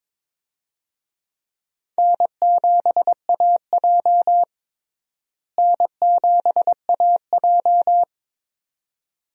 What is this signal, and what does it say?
Morse code, heard as a keyed tone, about 22 wpm, 700 Hz. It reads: N7AJ N7AJ
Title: USA Call Signs - 4 Characters - Repeated 2x - 22wpm